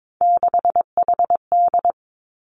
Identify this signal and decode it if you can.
Morse code, heard as a keyed tone, about 22 wpm, 700 Hz. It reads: 6HD
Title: Three Challenging Letter-Number Combinations 22wpm